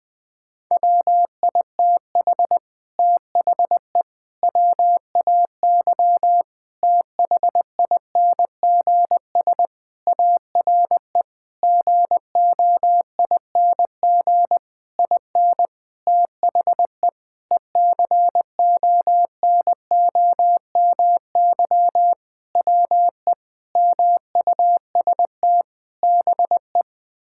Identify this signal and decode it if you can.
Morse code, heard as a keyed tone, about 19 wpm, 700 Hz. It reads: WITH THE WAY THINGS ARE GOING IN THE ECONOMY WE MUST BE
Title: Sentences with 8-Word Phrases - 20wpm